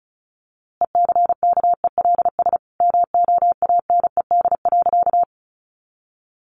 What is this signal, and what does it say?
Morse code, heard as a keyed tone, about 34 wpm, 700 Hz. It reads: ECKELS MOANED.
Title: A Sound Of Thunder 35wpm